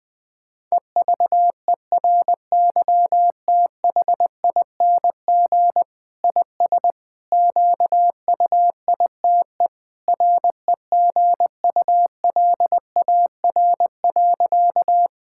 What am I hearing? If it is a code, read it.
Morse code, heard as a keyed tone, about 21 wpm, 700 Hz. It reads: EVERYTHING IS QUITE REGULAR.
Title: Sentences from Top 1000 Words - Review Part 6 - 20wpm